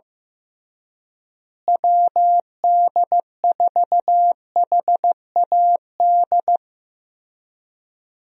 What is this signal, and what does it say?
Morse code, heard as a keyed tone, about 15 wpm, 700 Hz. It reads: WD4HAD
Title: USA Call Signs - 6 Characters - Repeated 3x - 15wpm